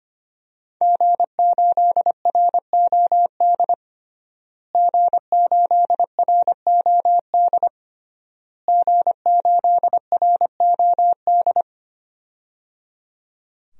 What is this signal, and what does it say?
Morse code, heard as a keyed tone, about 25 wpm, 700 Hz. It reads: G8ROD G8ROD G8ROD
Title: DXCC-100 Call Signs - 5 Characters - Repeated 3x - 25wpm